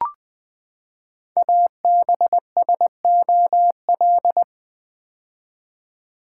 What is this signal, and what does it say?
Morse code, heard as a keyed tone, about 20 wpm, 700 Hz. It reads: ABSOL
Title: Build A Word - 5 Letters At A Time - 10-Letter Words - 20wpm